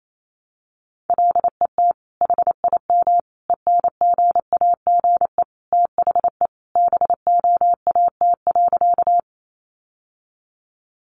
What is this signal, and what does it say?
Morse code, heard as a keyed tone, about 28 wpm, 700 Hz. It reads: LET HIM ENGAGE THE BOAT.
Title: Sentences from Top 1500 Words 28wpm